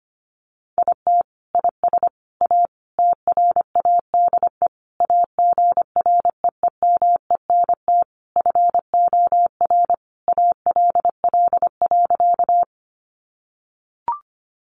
Morse code, heard as a keyed tone, about 25 wpm, 700 Hz. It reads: IT IS A TRADE AGREEMENT FOR ALL.
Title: Sentences from Top 1500 Words 25wpm